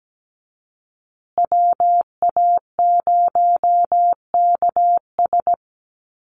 Morse code, heard as a keyed tone, about 17 wpm, 700 Hz. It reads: WA0KS
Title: USA Call Signs Repeated 3x 17wpm